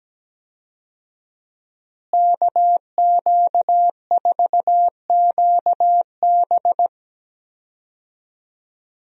Morse code, heard as a keyed tone, about 17 wpm, 700 Hz. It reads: KQ4QB
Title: USA Call Signs - 5 Characters - Repeated 2x - 17wpm